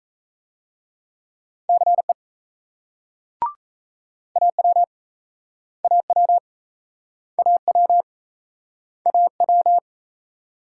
Morse code, heard as a keyed tone, about 39 wpm, 700 Hz. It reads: CE AW AW AW AW